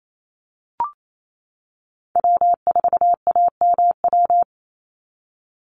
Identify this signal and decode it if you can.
Morse code, heard as a keyed tone, about 28 wpm, 700 Hz. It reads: W4AMW
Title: NA SOTA Chaser Callsigns 28wpm